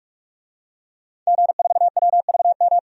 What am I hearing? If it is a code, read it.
Morse code, heard as a keyed tone, about 44 wpm, 700 Hz. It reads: G4WVM
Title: DXCC-100 Call Signs - 5 Characters - 45wpm